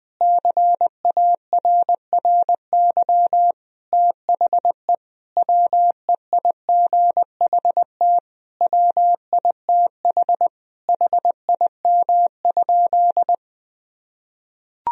Morse code, heard as a keyed tone, about 20 wpm, 700 Hz. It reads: CARRY THE WEIGHT WITH HIM?
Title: Sentences from Top 1000 Words - Review Part 3 - 20wpm